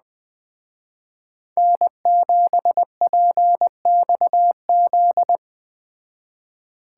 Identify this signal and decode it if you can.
Morse code, heard as a keyed tone, about 20 wpm, 700 Hz. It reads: N7PXZ
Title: Speed Racing - USA Call Signs 17wpm